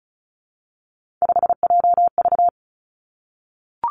Morse code, heard as a keyed tone, about 36 wpm, 700 Hz. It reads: 5JV
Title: Three Challenging Letter-Number Combinations 35wpm